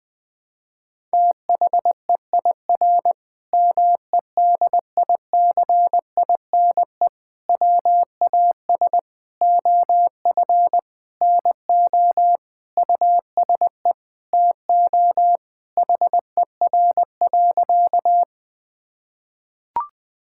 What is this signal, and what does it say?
Morse code, heard as a keyed tone, about 20 wpm, 700 Hz. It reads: THEIR MEDICINE WAS OF NO USE TO HER.
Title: Sentences from Top 1000 Words - Review Part 6 - 20wpm